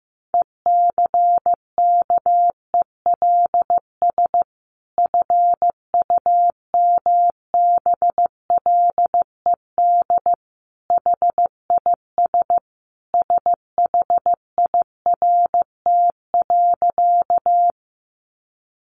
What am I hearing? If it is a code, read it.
Morse code, heard as a keyed tone, about 15 wpm, 700 Hz. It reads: ECKELS FUMBLED HIS SHIRT.
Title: A Sound Of Thunder 15wpm